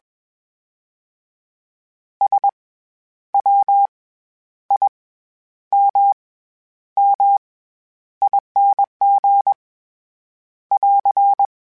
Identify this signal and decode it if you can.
Morse code, heard as a keyed tone, about 20 wpm, spaced as for 10 wpm, 800 Hz. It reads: SWIMMING <AR>
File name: Special INGs-mp3s / Special INGs-mp3s / SlowToFaster ing @20